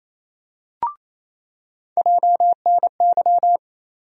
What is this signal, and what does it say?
Morse code, heard as a keyed tone, about 28 wpm, 700 Hz. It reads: JNY
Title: USA Call Sign Postfixes - 3 Letters 28wpm